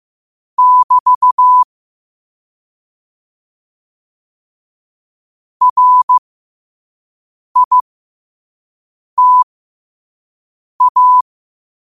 Morse code, heard as a keyed tone, about 15 wpm, spaced as for 5 wpm, 1000 Hz. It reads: <BT> RITA
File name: 5wpm